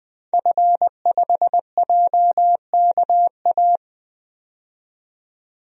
Morse code, heard as a keyed tone, about 20 wpm, 700 Hz. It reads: F5JKA
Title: DXCC-100 Call Signs - 5 Characters - 20wpm